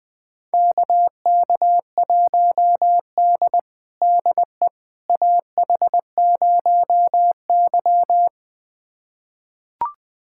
Morse code, heard as a keyed tone, about 20 wpm, 700 Hz. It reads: KK1D DE AH0Y
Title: USA Callsign DE Callsign - 4 Characters - 20wpm